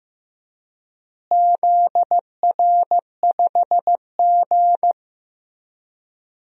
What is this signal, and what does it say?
Morse code, heard as a keyed tone, about 15 wpm, 700 Hz. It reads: ZR5G
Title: DXCC-100 Call Signs - 4 Characters - Repeated 3x - 15wpm